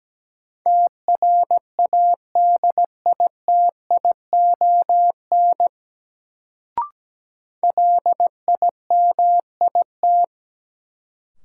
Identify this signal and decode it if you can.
Morse code, heard as a keyed tone, about 17 wpm, 700 Hz. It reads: TRADITION LIMIT
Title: Sentences from Top 1500 Words - New Word Drill Alt - 17wpm